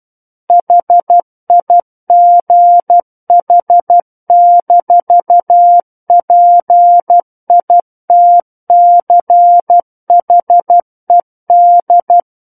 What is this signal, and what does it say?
Morse code, heard as a keyed tone, about 12 wpm, 700 Hz. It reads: HIGH-PITCHED
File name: Book One / 03-Chapter 3 - 12 WPM